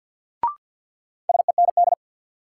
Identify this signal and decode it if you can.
Morse code, heard as a keyed tone, about 47 wpm, 700 Hz. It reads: SEND
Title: Sentences from Top 1500 Words - New Word Drill Alt - 50wpm